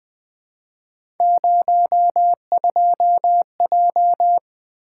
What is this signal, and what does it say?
Morse code, heard as a keyed tone, about 20 wpm, 700 Hz. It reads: 02J
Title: Three Letter-Number Combinations - 20wpm